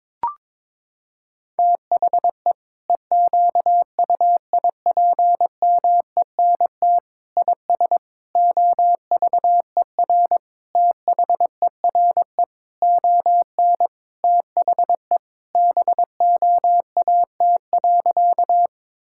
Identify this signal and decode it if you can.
Morse code, heard as a keyed tone, about 22 wpm, 700 Hz. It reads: THE EQUIPMENT IS OVER THERE ON THE BOAT.
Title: Sentences from Top 1000 Words - Review Part 5 - 22wpm